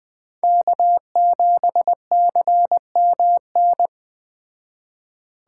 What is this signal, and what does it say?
Morse code, heard as a keyed tone, about 20 wpm, 700 Hz. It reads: K7CMN
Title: Speed Racing - USA 1x3 Call Signs 17wpm